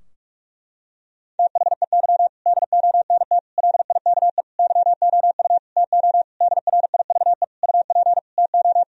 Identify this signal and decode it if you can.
Morse code, heard as a keyed tone, about 46 wpm, 700 Hz. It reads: THEY DONT LIKE YOU TO DRIVE UP TO
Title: Sentences from Top 1000 Words - Review Part 2 - 45wpm